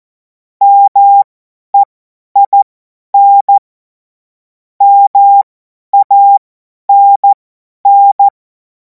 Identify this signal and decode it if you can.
Morse code, heard as a keyed tone, about 14 wpm, spaced as for 10 wpm, 800 Hz. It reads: MEIN MANN
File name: Morsekurs / Witze / Witz 24. Februar 2026 50 F70